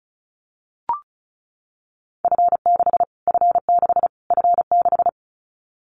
Morse code, heard as a keyed tone, about 34 wpm, 700 Hz. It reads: F6 F6 F6
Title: Callsign Prefixes for Top 25 DXCC - Repeated 3x - 35wpm